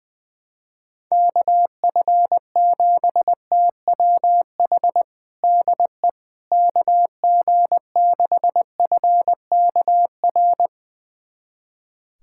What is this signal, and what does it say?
Morse code, heard as a keyed tone, about 20 wpm, 700 Hz. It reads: KF7TWH DE KG6FKR
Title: USA Callsign DE Callsign - 6 Characters - 20wpm